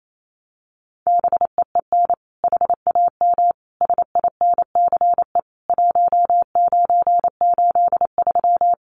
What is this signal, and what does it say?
Morse code, heard as a keyed tone, about 28 wpm, 700 Hz. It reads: BEEN HAM SINCE 1983
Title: Common QSO Phrases 28wpm